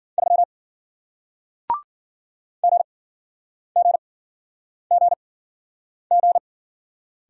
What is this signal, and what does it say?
Morse code, heard as a keyed tone, about 54 wpm, spaced as for 9 wpm, 700 Hz. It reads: 3 GGGG